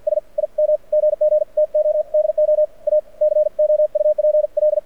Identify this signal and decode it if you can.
Morse code, heard as a keyed tone, about 48 wpm, 600 Hz. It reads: S IM GG TO DO A COUPL
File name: archived / CRSnet